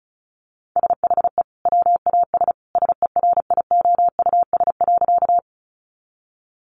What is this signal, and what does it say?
Morse code, heard as a keyed tone, about 34 wpm, 700 Hz. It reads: SHE WAS SERIOUS.
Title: Sentences from Top 1100 Words 35wpm